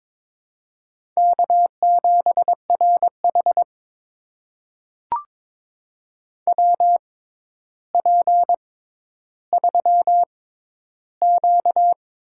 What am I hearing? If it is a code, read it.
Morse code, heard as a keyed tone, about 22 wpm, spaced as for 11 wpm, 700 Hz. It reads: K7RH WP3Q
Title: USA Call Signs - 4 Characters - 6x Character Spacing - 22wpm